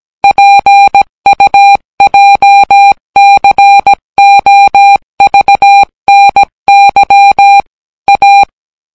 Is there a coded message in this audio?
Yes: Morse code from 0.2 to 8.4 s, heard as a keyed tone, about 17 wpm, 800 Hz. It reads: PUJCOVNY A